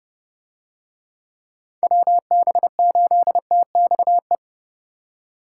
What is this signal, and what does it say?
Morse code, heard as a keyed tone, about 32 wpm, 700 Hz. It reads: WB8TXE